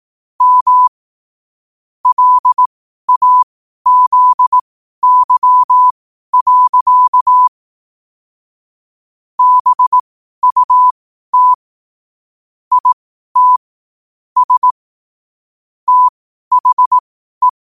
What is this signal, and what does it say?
Morse code, heard as a keyed tone, about 18 wpm, spaced as for 12 wpm, 1000 Hz. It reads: M LAZY. BUT IT S THE
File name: qotdfiles / 12wpm